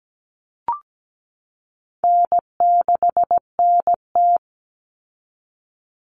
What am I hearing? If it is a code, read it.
Morse code, heard as a keyed tone, about 17 wpm, 700 Hz. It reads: N6NT
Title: Common Cut Numbers - 4-Digit 17wpm